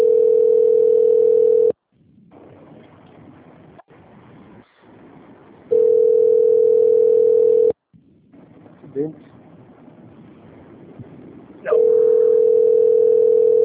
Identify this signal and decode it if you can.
Ringback tone (North America), 440 + 480 Hz, with speech.
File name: out- - - - - .12125